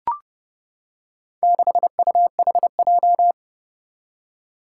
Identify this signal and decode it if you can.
Morse code, heard as a keyed tone, about 30 wpm, 700 Hz. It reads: 6UHJ
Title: Four Challenging Letter-Number Combinations 30wpm